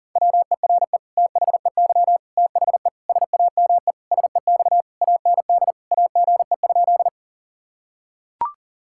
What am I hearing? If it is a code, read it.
Morse code, heard as a keyed tone, about 40 wpm, 700 Hz. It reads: WERE THEY THE SAME SEX AND AGE?
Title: Sentences from Top 1600 Words 40wpm